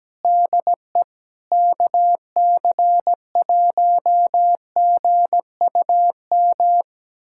Morse code, heard as a keyed tone, about 17 wpm, 700 Hz. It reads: DE KC1GUM
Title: USA Callsign DE Callsign - 6 Characters - 17wpm